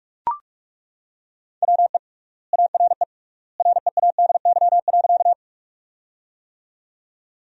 Morse code, heard as a keyed tone, about 42 wpm, 700 Hz. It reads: WE ARE READY.
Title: 3x Word-Spacing - Sentences from Top 400 Words - 45wpm